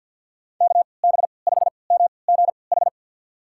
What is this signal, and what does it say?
Morse code, heard as a keyed tone, about 50 wpm, spaced as for 28 wpm, 700 Hz. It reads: KB5MGH
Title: USA Call Signs - 6 Characters - 3x Character Spacing - 50wpm